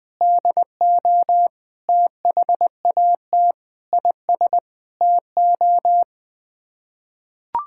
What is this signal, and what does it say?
Morse code, heard as a keyed tone, about 20 wpm, 700 Hz. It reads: DO THAT IS TO
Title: Sets of 8 Words - 20wpm